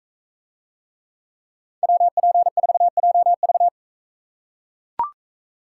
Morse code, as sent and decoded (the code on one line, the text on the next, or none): .-- .--. ....- .--- ...-
WP4JV